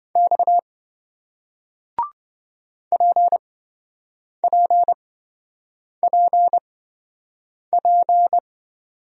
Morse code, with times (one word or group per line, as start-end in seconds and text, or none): 0.2-0.6 X
2.9-8.4 PPPP